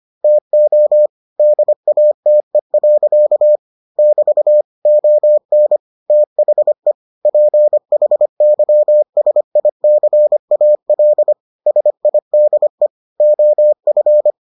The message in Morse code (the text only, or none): TO DATE. <BT> ON THE PHYSICAL SIDE OF